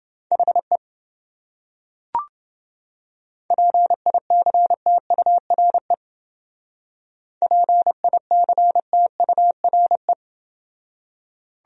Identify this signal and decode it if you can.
Morse code, heard as a keyed tone, about 29 wpm, 700 Hz. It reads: HE PICTURE PICTURE